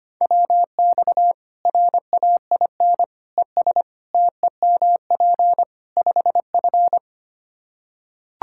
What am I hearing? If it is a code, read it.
Morse code, heard as a keyed tone, about 25 wpm, 700 Hz. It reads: WX RAIN ES TEMP 5F
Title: K1USN SST QSO Practice 25wpm